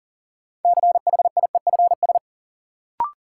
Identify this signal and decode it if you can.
Morse code, heard as a keyed tone, about 40 wpm, 700 Hz. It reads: CHIEFS
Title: Sets of 5 Words Encore - New Word Drill - 40wpm